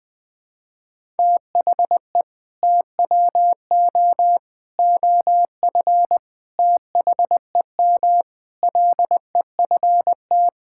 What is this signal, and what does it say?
Morse code, heard as a keyed tone, about 21 wpm, 700 Hz. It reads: THE TWO OF THEM LEFT
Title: Sentences with 4-Word Phrases - 20wpm